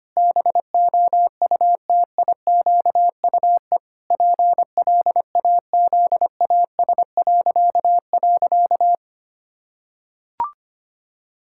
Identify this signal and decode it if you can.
Morse code, heard as a keyed tone, about 25 wpm, 700 Hz. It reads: BOUTIQUE PLAZAS..
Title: Pangram Challenge v2 - 25wpm